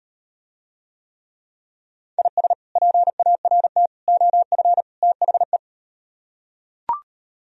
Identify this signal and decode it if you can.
Morse code, heard as a keyed tone, about 38 wpm, 700 Hz. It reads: IS PART OF THE